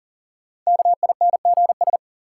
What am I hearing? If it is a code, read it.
Morse code, heard as a keyed tone, about 40 wpm, 700 Hz. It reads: KINGS
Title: Sentences from Top 1000 Words - Review Part 6 - New Word Drill - 40wpm